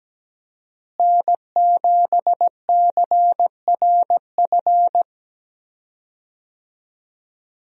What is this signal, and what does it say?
Morse code, heard as a keyed tone, about 17 wpm, 700 Hz. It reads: N7CRF
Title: USA Call Signs - 5 Characters - Repeated 3x - 17wpm